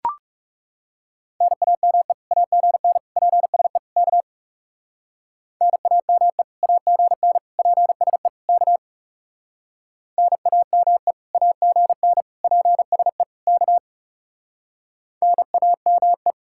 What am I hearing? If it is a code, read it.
Morse code, heard as a keyed tone, about 38 wpm, 700 Hz. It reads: NAME AGN PSE K NAME AGN PSE K NAME AGN PSE K NAME